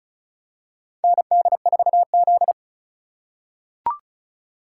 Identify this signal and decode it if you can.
Morse code, heard as a keyed tone, about 36 wpm, 700 Hz. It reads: ND4Z
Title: USA Call Signs - 4x Character Spacing - 35wpm